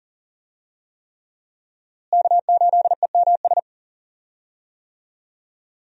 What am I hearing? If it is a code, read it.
Morse code, heard as a keyed tone, about 40 wpm, 700 Hz. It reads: K8EMS
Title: USA Call Signs - 5 Characters - Repeated 3x - 40wpm